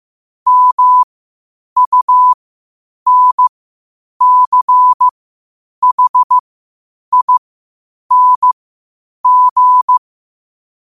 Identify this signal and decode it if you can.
Morse code, heard as a keyed tone, about 15 wpm, spaced as for 8 wpm, 1000 Hz. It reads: MUNCHING
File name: qotdfiles / 8wpm